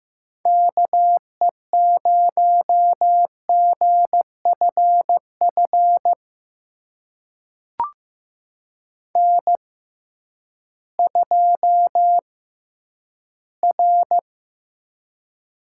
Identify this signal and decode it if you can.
Morse code, heard as a keyed tone, about 15 wpm, spaced as for 10 wpm, 700 Hz. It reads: KE0GFF N2R
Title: USA Call Signs - 6x Character Spacing - 15wpm